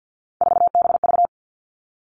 Morse code, heard as a keyed tone, about 50 wpm, 700 Hz. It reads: 4BV
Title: Three Challenging Letter-Number Combinations 50wpm